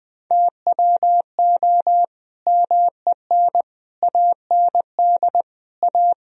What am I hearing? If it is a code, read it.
Morse code, heard as a keyed tone, about 21 wpm, 700 Hz. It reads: TWO MEN AND A
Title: Sentences from Top 1000 Words - Review Part 3 - 20wpm